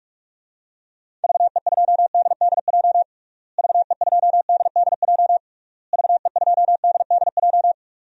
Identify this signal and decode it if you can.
Morse code, heard as a keyed tone, about 46 wpm, 700 Hz. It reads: VE2DDJ VE2DDJ VE2DDJ
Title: DXCC-100 Call Signs - 6 Characters - Repeated 3x - 45wpm